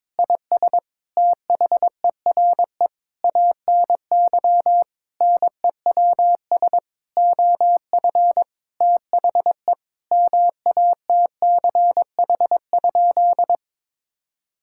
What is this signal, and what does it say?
Morse code, heard as a keyed tone, about 22 wpm, 700 Hz. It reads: IS THERE ANY NEWS OF THE MATCH?
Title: Sentences from Top 1600 Words 22wpm